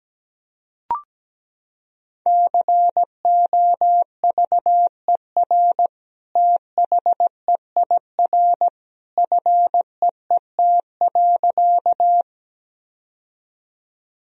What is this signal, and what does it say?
Morse code, heard as a keyed tone, about 17 wpm, 700 Hz. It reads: COVER THEIR FEET.
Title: Sentences from Top 1000 Words - Review Part 1 - 17wpm